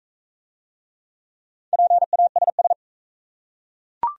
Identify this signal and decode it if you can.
Morse code, heard as a keyed tone, about 40 wpm, 690 Hz. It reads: PASS